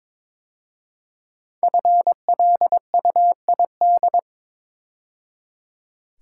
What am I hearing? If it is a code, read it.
Morse code, heard as a keyed tone, about 22 wpm, 700 Hz. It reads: FLUID
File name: 5 Letter Words - First 3 Letters Given - 22wpm